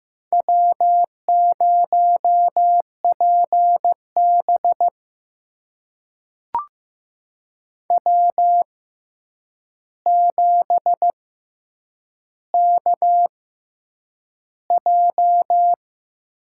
Morse code, heard as a keyed tone, about 15 wpm, spaced as for 8 wpm, 700 Hz. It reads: W0PB W7KJ
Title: USA Call Signs - 6x Character Spacing - 15wpm